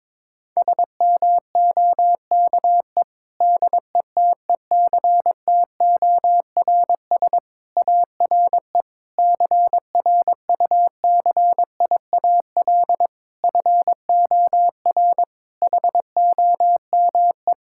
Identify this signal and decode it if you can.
Morse code, heard as a keyed tone, about 22 wpm, 700 Hz. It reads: SMOKE DETECTORS ARE CRUCIAL FOR HOME
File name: Difficult Sentences with Hints - 22wpm